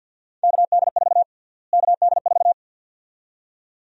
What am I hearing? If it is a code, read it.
Morse code, heard as a keyed tone, about 52 wpm, 700 Hz. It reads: KD4 KD4
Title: USA Call Sign Prefixes - Repeated 2x - 50wpm